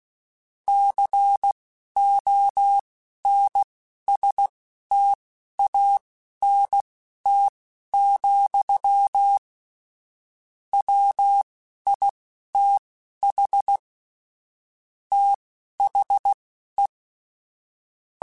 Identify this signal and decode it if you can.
Morse code, heard as a keyed tone, about 16 wpm, spaced as for 11 wpm, 800 Hz. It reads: CONSTANT, WITH THE
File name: Technical Subjects / Technical Subjects-mp3s / XC @10